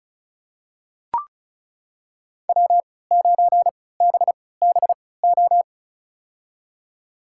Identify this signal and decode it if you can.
Morse code, heard as a keyed tone, about 36 wpm, spaced as for 20 wpm, 700 Hz. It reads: W9BBO
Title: USA Call Signs - 5 Characters - 3x Character Spacing - 35wpm